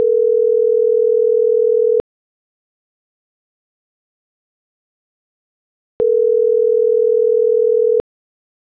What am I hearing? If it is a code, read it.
Ringback tone (North America), 440 + 480 Hz.